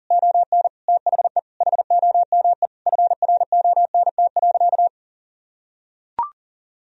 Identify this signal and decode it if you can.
Morse code, heard as a keyed tone, about 40 wpm, 700 Hz. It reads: ON THE HOME FRONT.
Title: Sentences from Top 1000 Words - Review Part 2 - 40wpm